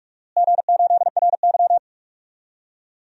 Morse code, heard as a keyed tone, about 43 wpm, 700 Hz. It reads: G8RY
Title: DXCC-100 Call Signs - 4 Characters - Repeated 3x - 45wpm